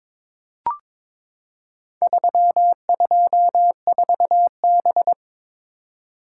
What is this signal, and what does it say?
Morse code, heard as a keyed tone, about 22 wpm, 700 Hz. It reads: 324B
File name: practice / 22wpm / Four Letter-Number Combinations - 22wpm